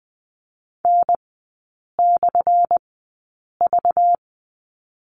Morse code, heard as a keyed tone, about 20 wpm, spaced as for 8 wpm, 700 Hz. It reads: N/V
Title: 191 Lesson 31e - Letter v - Three Characters 20wpm